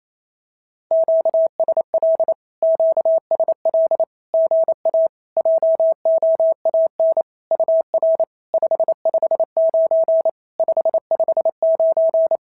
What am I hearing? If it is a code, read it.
Morse code, heard as a keyed tone, about 28 wpm, 650 Hz. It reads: QSL QSL GA JOAN UR 559 559
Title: SOTA QSO Practice 28wpm